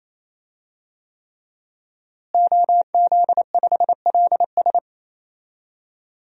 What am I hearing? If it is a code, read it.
Morse code, heard as a keyed tone, about 28 wpm, 700 Hz. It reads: OZ5LS
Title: DXCC-100 Call Signs - 5 Characters - Repeated 3x - 28wpm